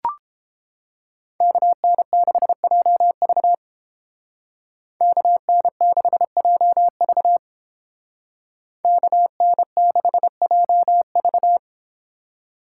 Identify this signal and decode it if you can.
Morse code, heard as a keyed tone, about 30 wpm, 700 Hz. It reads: KN6JV KN6JV KN6JV